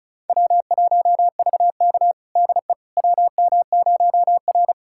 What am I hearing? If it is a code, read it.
Morse code, heard as a keyed tone, about 34 wpm, 700 Hz. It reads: W1VK DE WM0R
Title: USA Callsign DE Callsign - 4 Characters - 35wpm